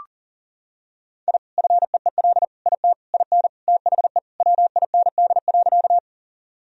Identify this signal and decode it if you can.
Morse code, heard as a keyed tone, about 40 wpm, 700 Hz. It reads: I FEEL IT IN THE WIND.
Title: Sentences from Top 1000 Words - Review Part 2 - 40wpm